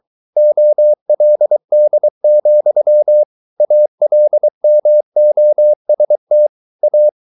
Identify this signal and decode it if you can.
Morse code, heard as a keyed tone, about 23 wpm, 600 Hz. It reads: OLD, ALMOST A